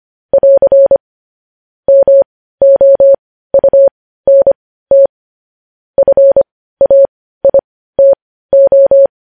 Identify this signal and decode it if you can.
Morse code, heard as a keyed tone, about 25 wpm, spaced as for 15 wpm, 550 Hz. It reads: <AR> MOUNT FAITO
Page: news-headlines-in-morse-code-at-15-wpm